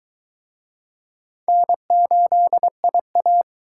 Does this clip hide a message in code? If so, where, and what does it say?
Yes: Morse code from 1.5 to 3.4 s, heard as a keyed tone, about 23 wpm, 700 Hz. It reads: N8IA